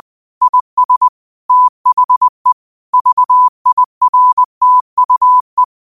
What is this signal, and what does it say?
Morse code, heard as a keyed tone, about 20 wpm, 1000 Hz. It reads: IS THE VIRTUE